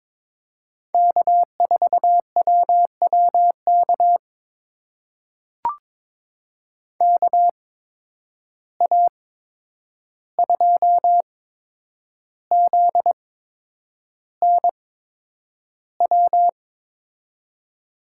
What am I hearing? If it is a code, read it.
Morse code, heard as a keyed tone, about 22 wpm, spaced as for 9 wpm, 700 Hz. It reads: K4WWK KA2ZNW
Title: USA Call Signs - 8x Character Spacing - 22wpm